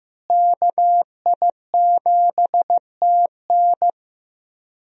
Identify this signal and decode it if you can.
Morse code, heard as a keyed tone, about 15 wpm, 700 Hz. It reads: KI7TN
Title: USA Call Signs - 5 Characters - Repeated 2x - 15wpm